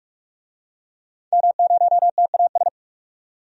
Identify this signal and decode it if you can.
Morse code, heard as a keyed tone, about 44 wpm, 700 Hz. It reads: M0TAS
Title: DXCC-100 Call Signs - 5 Characters - 45wpm